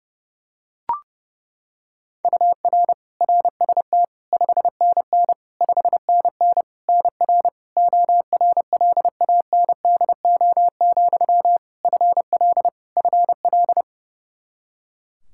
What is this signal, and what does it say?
Morse code, heard as a keyed tone, about 30 wpm, 700 Hz. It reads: UR RST 5NN 5NN NR ORLANDO, FL FL
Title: Practice QSO - Exchange 1 - Beginning - 30wpm